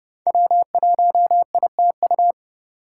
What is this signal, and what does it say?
Morse code, heard as a keyed tone, about 30 wpm, 700 Hz. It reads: W1ITU
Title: USA Call Signs - 5 Characters - 6x Character Spacing - 30wpm